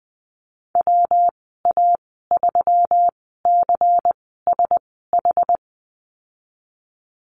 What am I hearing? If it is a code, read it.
Morse code, heard as a keyed tone, about 20 wpm, spaced as for 15 wpm, 700 Hz. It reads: WA3CSH
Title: 122 Lesson 20g - Number 3 - Call Signs 20wpm